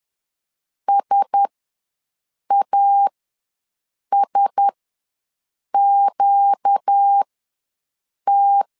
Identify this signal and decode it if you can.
Morse code, heard as a keyed tone, about 10 wpm, spaced as for 6 wpm, 800 Hz. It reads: SASQT
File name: LEZIONI / 4 Q1S ws 10wpm 2208